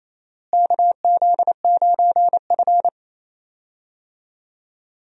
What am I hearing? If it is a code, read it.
Morse code, heard as a keyed tone, about 28 wpm, 700 Hz. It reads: KZ9F
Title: USA Call Signs - 4 Characters - Repeated 2x - 28wpm